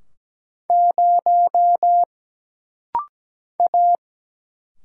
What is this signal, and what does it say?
Morse code, heard as a keyed tone, about 17 wpm, 700 Hz. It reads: 0 A
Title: Single Letter-Number - Rapid-Fire 17wpm